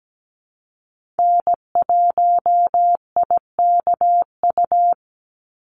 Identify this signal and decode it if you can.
Morse code, heard as a keyed tone, about 17 wpm, 700 Hz. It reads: N1IKU
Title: USA Call Signs 17wpm